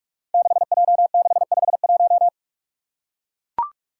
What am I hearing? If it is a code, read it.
Morse code, heard as a keyed tone, about 44 wpm, 700 Hz. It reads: 6J651
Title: Five Challenging Letter-Number Combinations 45wpm